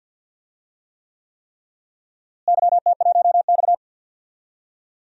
Morse code, heard as a keyed tone, about 49 wpm, 700 Hz. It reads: YT1X
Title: DXCC-100 Call Signs - 4 Characters - Repeated 3x - 50wpm